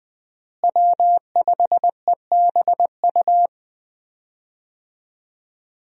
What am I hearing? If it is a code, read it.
Morse code, heard as a keyed tone, about 20 wpm, 700 Hz. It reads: W5EBU
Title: USA Call Signs - 5 Characters - Repeated 2x - 20wpm